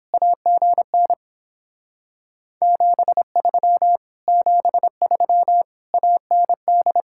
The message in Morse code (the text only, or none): AGN 73 73 AND